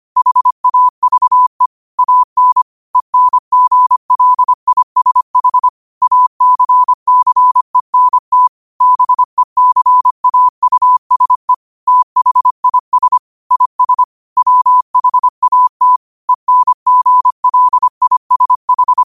Morse code, heard as a keyed tone, about 25 wpm, 1000 Hz. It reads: HAVE AN ENGLISH ACCENT BECAUSE THIS IS WHAT ENGLISH
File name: qotdfiles / 25wpm